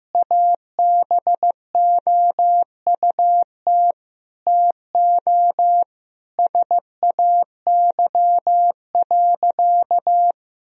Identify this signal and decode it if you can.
Morse code, heard as a keyed tone, about 15 wpm, 700 Hz. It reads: ABOUT TO SAY.
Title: Sentences from Top 1000 Words - Review Part 2 - 15wpm